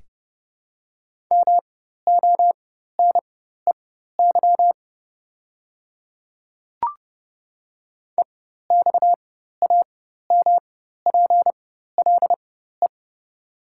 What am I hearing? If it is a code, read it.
Morse code, heard as a keyed tone, about 30 wpm, spaced as for 14 wpm, 690 Hz. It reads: MONEY EXAMPLE
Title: 4x Character Spacing - Top 500 Words - 30wpm